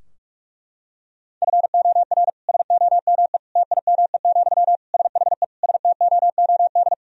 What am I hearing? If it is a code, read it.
Morse code, heard as a keyed tone, about 46 wpm, 700 Hz. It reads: FOR SOME TIME, SHE STOOD
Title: Sentences from Top 1000 Words - Review Part 2 - 45wpm